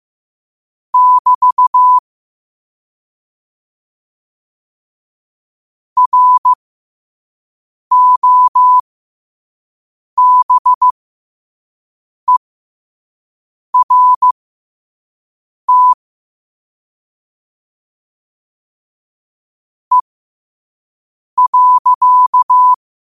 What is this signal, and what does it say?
Morse code, heard as a keyed tone, about 15 wpm, spaced as for 5 wpm, 1000 Hz. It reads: <BT> ROBERT E.
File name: qotdfiles / 5wpm